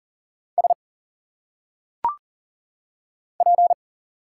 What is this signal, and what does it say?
Morse code, heard as a keyed tone, about 40 wpm, 700 Hz. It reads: S P
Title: Single Letters - Phonetic Pronunciation 40wpm